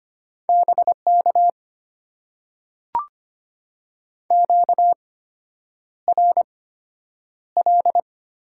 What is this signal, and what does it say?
Morse code, heard as a keyed tone, about 25 wpm, spaced as for 9 wpm, 700 Hz. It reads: BK QRL
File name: Top 90 QSO Elements - 8x Character Spacing - 25wpm